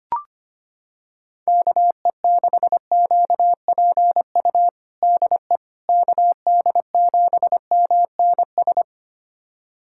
Morse code, heard as a keyed tone, about 25 wpm, 700 Hz. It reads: KE6QPU DE KD7MNS
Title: USA Callsign DE Callsign 25wpm